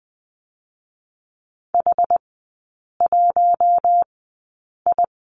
Morse code, heard as a keyed tone, about 20 wpm, spaced as for 8 wpm, 700 Hz. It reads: H1I
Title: 060 Lesson 11e - Letter h - Three Characters 20wpm